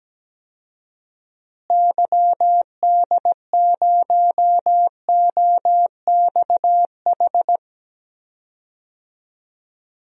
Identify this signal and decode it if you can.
Morse code, heard as a keyed tone, about 17 wpm, 700 Hz. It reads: YD0OXH
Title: DXCC-100 Call Signs - 6 Characters - Repeated 2x - 17wpm